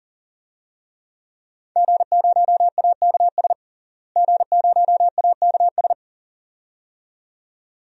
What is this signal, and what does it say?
Morse code, heard as a keyed tone, about 38 wpm, 700 Hz. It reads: G0AKS G0AKS
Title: DXCC-100 Call Signs - 5 Characters - Repeated 2x - 40wpm